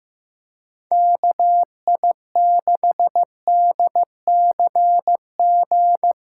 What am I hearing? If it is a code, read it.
Morse code, heard as a keyed tone, about 15 wpm, 700 Hz. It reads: KI6DCG
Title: USA Callsign DE Callsign - 6 Characters - 15wpm